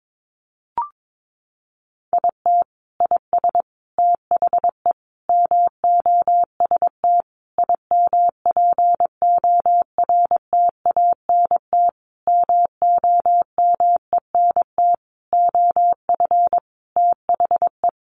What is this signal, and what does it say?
Morse code, heard as a keyed tone, about 22 wpm, 700 Hz. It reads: IT IS THE MOST IMPORTANT MOMENT OF THE
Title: Sentences from Top 1500 Words 22wpm